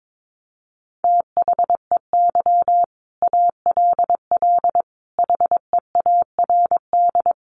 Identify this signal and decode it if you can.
Morse code, heard as a keyed tone, about 22 wpm, 700 Hz. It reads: THEY ALL HEARD
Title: Sentences from Top 400 Words 22wpm